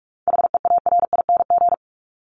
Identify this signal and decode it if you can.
Morse code, heard as a keyed tone, about 44 wpm, 700 Hz. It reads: HEARING
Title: ING Suffix - Top 100 Words with Base Words 45wpm